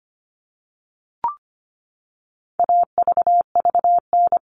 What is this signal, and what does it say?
Morse code, heard as a keyed tone, about 25 wpm, 700 Hz. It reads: AVVN
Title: All Cut Numbers - 4-Digit 25wpm